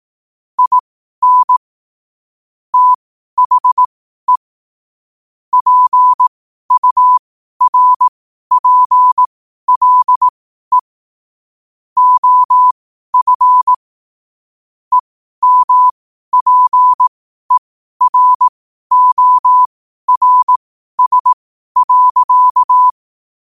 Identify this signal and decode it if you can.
Morse code, heard as a keyed tone, about 18 wpm, spaced as for 12 wpm, 1000 Hz. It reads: IN THE PURPLE OF EMPERORS.
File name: qotdfiles / 12wpm